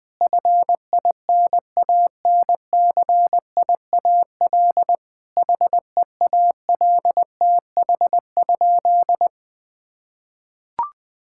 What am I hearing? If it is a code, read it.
Morse code, heard as a keyed tone, about 21 wpm, 700 Hz. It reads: FINANCIAL HEALTH?
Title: Sentences from Top 1000 Words - Review Part 5 - 20wpm